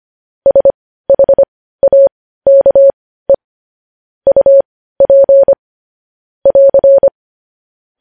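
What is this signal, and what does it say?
Morse code, heard as a keyed tone, about 25 wpm, spaced as for 15 wpm, 550 Hz. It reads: SHAKE UP <AR>
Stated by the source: News headlines sent in Morse code at 15 words per minute, preceded by vvv.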